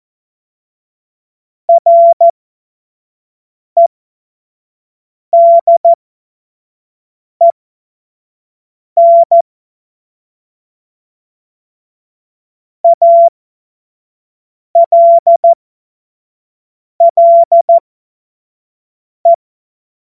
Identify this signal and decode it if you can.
Morse code, heard as a keyed tone, about 14 wpm, spaced as for 5 wpm, 680 Hz. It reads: REDEN ALLE
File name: Morsekurs / Witze / Witz 06. Juli 2021, 25 F70